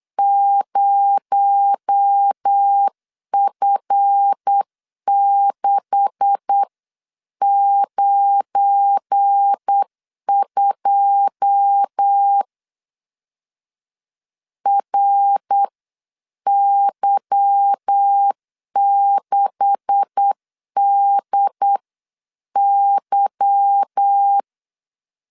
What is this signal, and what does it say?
Morse code, heard as a keyed tone, about 8 wpm, spaced as for 7 wpm, 800 Hz. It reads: 0F692 RY6DY
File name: LEZIONI / 9 286 8wpm 3425